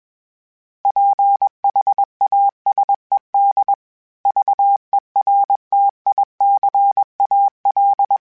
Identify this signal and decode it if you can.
Morse code, heard as a keyed tone, about 21 wpm, 800 Hz. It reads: PHASED VERTICAL
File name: fromSMs / FromSMs mp3s / Antennas@20